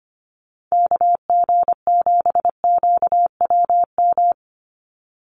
Morse code, heard as a keyed tone, about 25 wpm, 700 Hz. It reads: KG7QWM
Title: USA Call Signs 25wpm